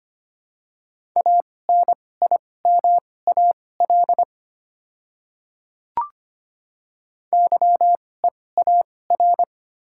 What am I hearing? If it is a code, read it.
Morse code, heard as a keyed tone, about 25 wpm, spaced as for 18 wpm, 700 Hz. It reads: ANIMAL YEAR
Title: 2x Character Spacing - Top 200 Words - 25wpm